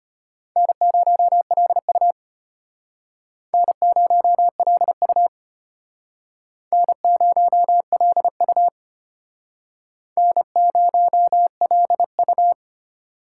Morse code, heard as a keyed tone, about 32 wpm, 700 Hz. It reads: N0LU N0LU N0LU N0LU